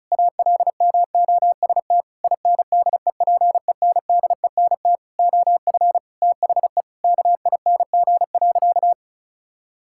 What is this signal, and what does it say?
Morse code, heard as a keyed tone, about 34 wpm, 700 Hz. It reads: ALMOST INDEPENDENT OF THE KING.
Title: Sentences from Top 1000 Words - Review Part 6 - 35wpm